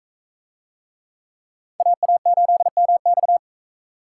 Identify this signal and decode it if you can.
Morse code, heard as a keyed tone, about 44 wpm, 700 Hz. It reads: AA8MX